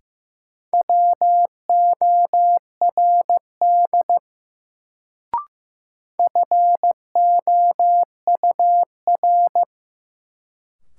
Morse code, heard as a keyed tone, about 15 wpm, 700 Hz. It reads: WORD FOUR
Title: Sentences from Top 300 Words - New Word Drill - 15wpm